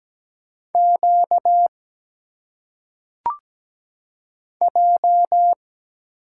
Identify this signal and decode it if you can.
Morse code, heard as a keyed tone, about 17 wpm, 700 Hz. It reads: Q J
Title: Single Letters - Phonetic Pronunciation 17wpm